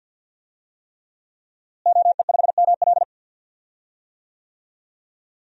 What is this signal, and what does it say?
Morse code, heard as a keyed tone, about 51 wpm, 700 Hz. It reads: OE5ML
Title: DXCC-100 Call Signs - 5 Characters - Repeated 3x - 50wpm